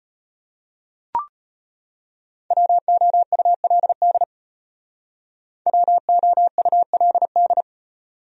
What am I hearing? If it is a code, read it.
Morse code, heard as a keyed tone, about 33 wpm, 700 Hz. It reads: WOULD WOULD